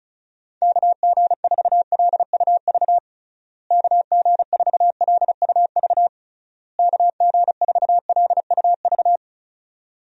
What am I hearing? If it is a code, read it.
Morse code, heard as a keyed tone, about 36 wpm, 700 Hz. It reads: KG4LUV KG4LUV KG4LUV
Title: USA Call Signs - 6 Characters - Repeated 3x - 35wpm